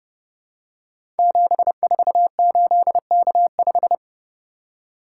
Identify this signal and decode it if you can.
Morse code, heard as a keyed tone, about 30 wpm, 700 Hz. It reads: 748K5
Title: Code Groups - 5 Letter:Numbers - 30wpm